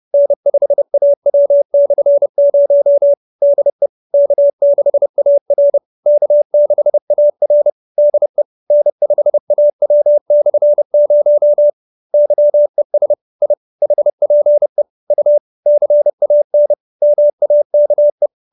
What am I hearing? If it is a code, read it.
Morse code, heard as a keyed tone, about 30 wpm, 570 Hz. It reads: N5AW/0 DE K6AR K6AR DE N5AW/0 YES I HPE U CAN MAKE